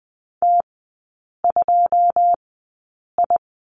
Morse code, heard as a keyed tone, about 20 wpm, spaced as for 8 wpm, 700 Hz. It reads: T2I
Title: 093 Lesson 16e - Letter c - Three Characters 20wpm